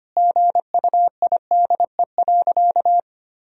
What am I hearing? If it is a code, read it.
Morse code, heard as a keyed tone, about 25 wpm, 700 Hz. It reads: GUIDE.
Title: Sentences from Top 1000 Words - Review Part 3 - 25wpm